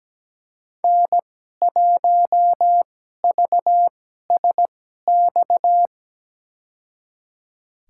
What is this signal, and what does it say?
Morse code, heard as a keyed tone, about 17 wpm, spaced as for 12 wpm, 700 Hz. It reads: N1VSX
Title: USA Call Signs - 2x Character Spacing - 17wpm